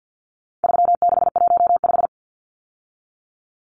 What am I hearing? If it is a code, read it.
Morse code, heard as a keyed tone, about 50 wpm, 700 Hz. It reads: 3615
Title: Numbers - 4-Digit 50wpm